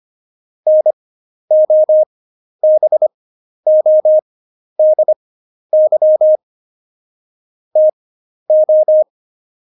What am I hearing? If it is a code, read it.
Morse code, heard as a keyed tone, about 25 wpm, spaced as for 11 wpm, 620 Hz. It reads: NOBODY TO
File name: Jokes-25-char-10-eff